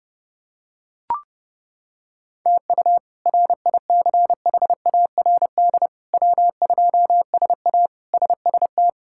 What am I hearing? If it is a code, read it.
Morse code, heard as a keyed tone, about 30 wpm, 700 Hz. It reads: TU RICHARD W2SA SST
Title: K1USN SST QSO Practice - 30wpm